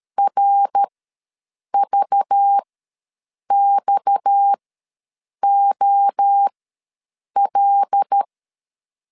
Morse code, heard as a keyed tone, about 13 wpm, spaced as for 7 wpm, 800 Hz. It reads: RVXOL
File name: LEZIONI / 6 PKX0 / 6 PKX0 ws 12wpm 3448